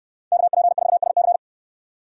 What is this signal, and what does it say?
Morse code, heard as a keyed tone, about 69 wpm, 700 Hz. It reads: KP4SC